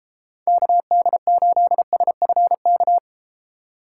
Morse code, heard as a keyed tone, about 32 wpm, 700 Hz. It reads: KD8SFK